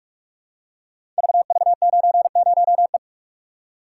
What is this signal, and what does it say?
Morse code, heard as a keyed tone, about 44 wpm, 700 Hz. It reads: VV90E
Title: Code Groups - 5 Letter:Numbers - 45wpm